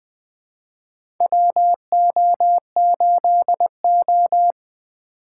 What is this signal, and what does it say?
Morse code, heard as a keyed tone, about 20 wpm, 700 Hz. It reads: WO8O
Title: USA Call Signs - 4 Characters - 2x Character Spacing - 20wpm